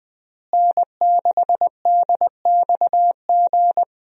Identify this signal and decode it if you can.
Morse code, heard as a keyed tone, about 20 wpm, 700 Hz. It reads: N6DXG
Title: USA Call Signs - 5 Characters - Repeated 3x - 20wpm